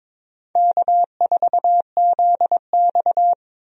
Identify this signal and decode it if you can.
Morse code, heard as a keyed tone, about 22 wpm, 700 Hz. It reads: K4ZX
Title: USA Call Signs - 4 Characters - Repeated 3x - 22wpm